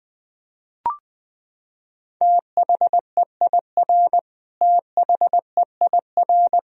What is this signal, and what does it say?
Morse code, heard as a keyed tone, about 21 wpm, 700 Hz. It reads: THEIR THEIR
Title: Top 100 Words Repeated 20wpm